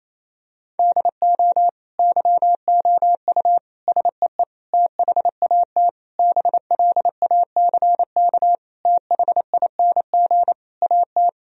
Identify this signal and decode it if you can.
Morse code, heard as a keyed tone, about 28 wpm, 700 Hz. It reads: DO YOU SEE THAT BLACK THING AT
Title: Sentences from Top 1000 Words - Review Part 2 - 28wpm